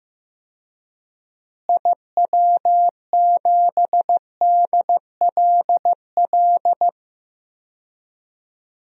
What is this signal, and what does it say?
Morse code, heard as a keyed tone, about 15 wpm, 700 Hz. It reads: IW7DLL
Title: DXCC-100 Call Signs - 6 Characters - Repeated 3x - 15wpm